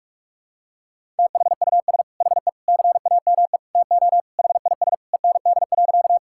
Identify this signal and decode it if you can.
Morse code, heard as a keyed tone, about 46 wpm, 700 Hz. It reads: THUS HE CAME TO HIS END.
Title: Sentences from Top 1000 Words - Review Part 5 - 45wpm